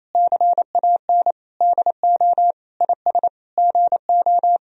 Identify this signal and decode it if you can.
Morse code, heard as a keyed tone, about 28 wpm, 700 Hz. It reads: CAN DO IS GO
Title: Sentences from Top 200 Words and Abbreviations 28wpm